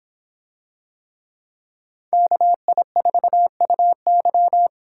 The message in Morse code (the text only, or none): KI4UY